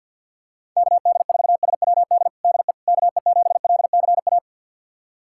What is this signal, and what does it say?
Morse code, heard as a keyed tone, about 53 wpm, 690 Hz. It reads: KD4SWD DE KE7LCA
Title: USA Callsign DE Callsign - 6 Characters - 50wpm